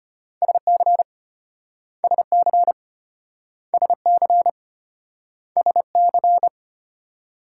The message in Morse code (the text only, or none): SC SC SC SC